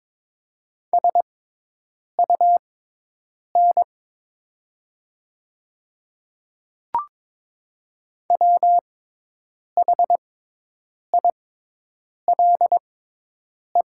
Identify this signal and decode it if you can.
Morse code, heard as a keyed tone, about 22 wpm, spaced as for 8 wpm, 700 Hz. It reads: SUN WHILE
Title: 6x Character Spacing - Top 300 Words - 22wpm